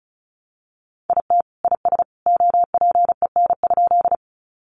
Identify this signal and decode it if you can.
Morse code, heard as a keyed tone, about 34 wpm, 700 Hz. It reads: IT IS OPEN?
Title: Sentences from Top 300 Words 35wpm